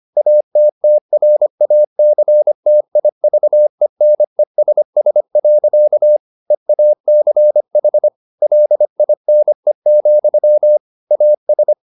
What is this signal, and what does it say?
Morse code, heard as a keyed tone, about 25 wpm, 600 Hz. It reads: ATTRACTIVENESS. EACH LINE, AS
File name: the old man and the sea part 1 25wpm